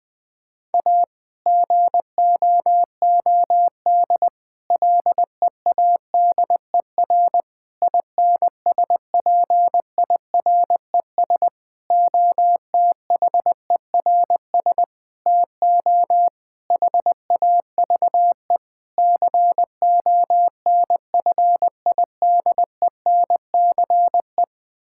Morse code, heard as a keyed tone, about 20 wpm, 700 Hz. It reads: A GOOD LEADER INSPIRES OTHERS TO HAVE CONFIDENCE
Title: Difficult Sentences with Hints - 20wpm